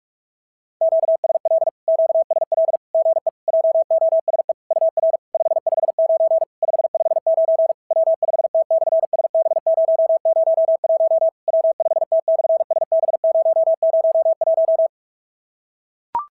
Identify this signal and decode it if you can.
Morse code, heard as a keyed tone, about 46 wpm, 650 Hz. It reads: QSL QSL GE JOSE UR 559 559 W5T/SB001 W5T/SB001
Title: SOTA QSO Practice 45wpm